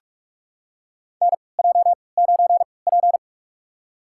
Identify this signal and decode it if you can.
Morse code, heard as a keyed tone, about 44 wpm, spaced as for 25 wpm, 700 Hz. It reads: NJ9P
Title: USA Call Signs - 3x Character Spacing - 45wpm